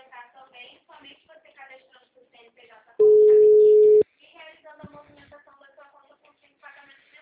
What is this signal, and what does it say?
Ringback tone (Europe, 425 Hz), with speech.